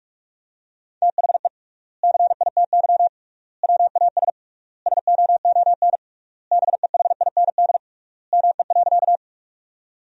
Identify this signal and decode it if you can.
Morse code, heard as a keyed tone, about 41 wpm, 700 Hz. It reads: THE CITY WAS SOON BEHIND ME.
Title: 3x Word-Spacing - Sentences from Top 500 Words - 45wpm